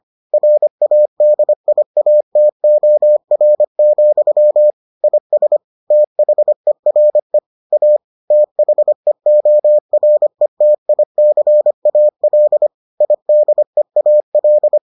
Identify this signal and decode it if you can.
Morse code, heard as a keyed tone, about 25 wpm, 600 Hz. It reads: RADIATOR, IS THERE A THEORETICAL IDEAL